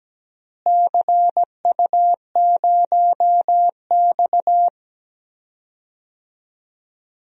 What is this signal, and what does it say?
Morse code, heard as a keyed tone, about 17 wpm, 700 Hz. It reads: CU0X
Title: DXCC-100 Call Signs - 4 Characters - Repeated 2x - 17wpm